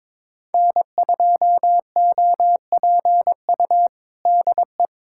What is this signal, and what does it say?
Morse code, heard as a keyed tone, about 22 wpm, 700 Hz. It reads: N2OPU DE
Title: USA Callsign DE Callsign 22wpm